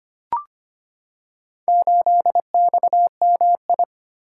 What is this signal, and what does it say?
Morse code, heard as a keyed tone, about 25 wpm, 700 Hz. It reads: 8XMI
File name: practice / 25wpm / Four Letter-Number Combinations - 25wpm